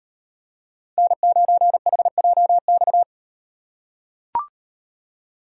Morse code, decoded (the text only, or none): N9HJX